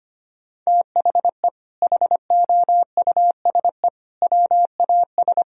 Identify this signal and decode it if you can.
Morse code, heard as a keyed tone, about 25 wpm, 700 Hz. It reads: THE HOUSE WAS
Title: Sentences from Top 1000 Words - Review Part 3 - 25wpm